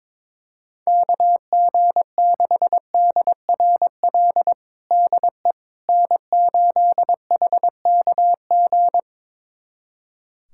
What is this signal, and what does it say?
Morse code, heard as a keyed tone, about 23 wpm, 700 Hz. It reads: KG6DRL DE N8HKG
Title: USA Callsign DE Callsign 22wpm